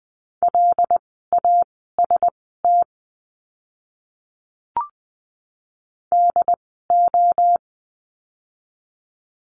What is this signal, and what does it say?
Morse code, heard as a keyed tone, about 20 wpm, spaced as for 14 wpm, 700 Hz. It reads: LAST DO
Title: 206 Lesson 33f - Letter j - Words 20wpm